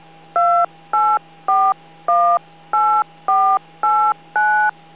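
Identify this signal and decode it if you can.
DTMF, heard as telephone touch tones, tones of about 310 ms. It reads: 28418489
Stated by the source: Autopatch tones used in a drill on making 9-1-1 calls through a repeater.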